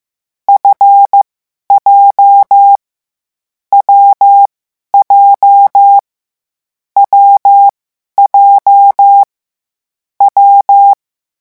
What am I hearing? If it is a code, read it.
Morse code, heard as a keyed tone, about 15 wpm, spaced as for 11 wpm, 800 Hz. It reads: FJ W J W J W
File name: morse mp3 / lecon7-vitesse 14